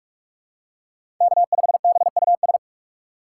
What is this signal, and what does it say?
Morse code, heard as a keyed tone, about 44 wpm, 700 Hz. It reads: K5BUS